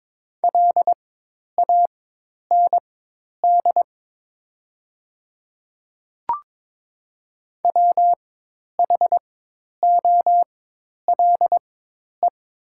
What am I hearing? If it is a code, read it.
Morse code, heard as a keyed tone, about 22 wpm, spaced as for 10 wpm, 690 Hz. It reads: LAND WHOLE
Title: 4x Character Spacing - Top 500 Words - 22wpm